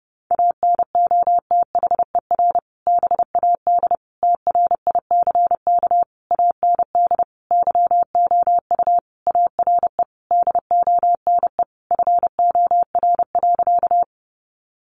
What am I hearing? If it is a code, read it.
Morse code, heard as a keyed tone, about 30 wpm, 700 Hz. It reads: ANOTHER BAD TRICK AND YOU ARE DONE FOR.
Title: Sentences from Top 1500 Words 30wpm